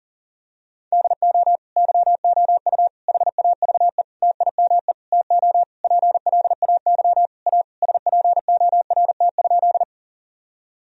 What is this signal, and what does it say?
Morse code, heard as a keyed tone, about 40 wpm, 700 Hz. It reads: DO YOU HAVE TIME TO PLAY A SPORT?
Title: Sentences from Top 1000 Words - Review Part 4 - 40wpm